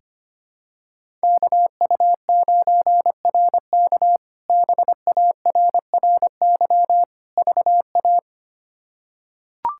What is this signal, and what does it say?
Morse code, heard as a keyed tone, about 25 wpm, 700 Hz. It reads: KU9RK BARRY VA
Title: K1USN SST QSO Practice - 25wpm